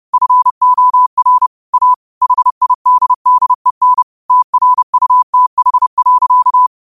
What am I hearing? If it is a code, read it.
Morse code, heard as a keyed tone, about 30 wpm, 1000 Hz. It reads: FOR A HIDDEN TRUTH.